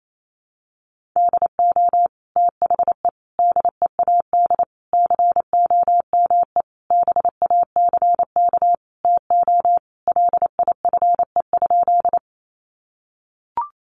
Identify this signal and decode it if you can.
Morse code, heard as a keyed tone, about 28 wpm, 700 Hz. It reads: DO THE DEAD COME BACK TO LIFE?
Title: Sentences from Top 1500 Words 28wpm